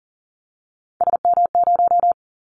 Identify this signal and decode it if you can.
Morse code, heard as a keyed tone, about 38 wpm, 700 Hz. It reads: SM0
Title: Callsign Prefixes for Top 25 DXCC - 40wpm